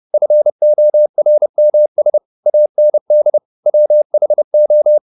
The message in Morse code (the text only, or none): FORMS AND WHO